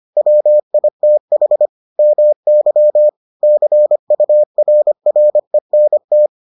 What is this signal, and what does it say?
Morse code, heard as a keyed tone, about 25 wpm, 600 Hz. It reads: WITH MY CURRENT